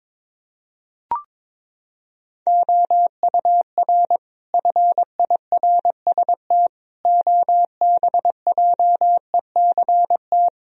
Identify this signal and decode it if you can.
Morse code, heard as a keyed tone, about 23 wpm, 700 Hz. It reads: OUR FIRST OBJECT
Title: Sentences from Top 1000 Words - Review Part 3 - 22wpm